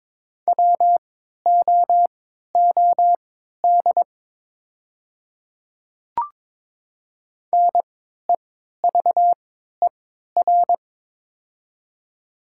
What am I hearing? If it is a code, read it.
Morse code, heard as a keyed tone, about 22 wpm, spaced as for 12 wpm, 700 Hz. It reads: WOOD NEVER
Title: 3x Character Spacing - Top 500 Words - 22wpm